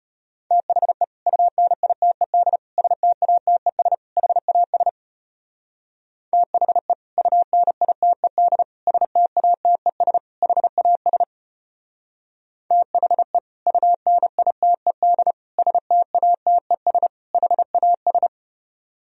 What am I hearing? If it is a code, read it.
Morse code, heard as a keyed tone, about 32 wpm, 700 Hz. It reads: THE UNITED STATES HAS THE UNITED STATES HAS THE UNITED STATES HAS